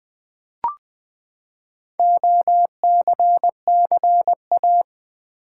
Morse code, heard as a keyed tone, about 20 wpm, 700 Hz. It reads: OCCA
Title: Build A Word - 4 Letters At A Time - 10-Letter Words - 20wpm